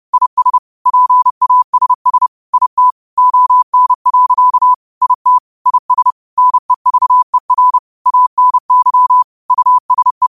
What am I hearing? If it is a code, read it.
Morse code, heard as a keyed tone, about 30 wpm, 1000 Hz. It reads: IS PASS IT ON. IT IS NEVER ANY USE
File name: qotdfiles / 30wpm